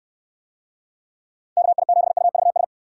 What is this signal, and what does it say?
Morse code, heard as a keyed tone, about 69 wpm, 700 Hz. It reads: KI7UVS